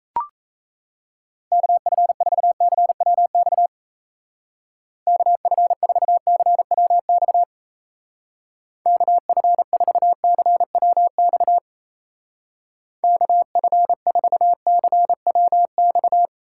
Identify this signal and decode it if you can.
Morse code, heard as a keyed tone, about 37 wpm, 700 Hz. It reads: KF4CWX KF4CWX KF4CWX KF4CWX